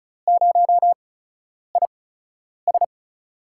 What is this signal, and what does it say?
Morse code, heard as a keyed tone, about 36 wpm, 700 Hz. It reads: 0 I S
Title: USA Call Signs - 4 Characters - 8x Character Spacing - 35wpm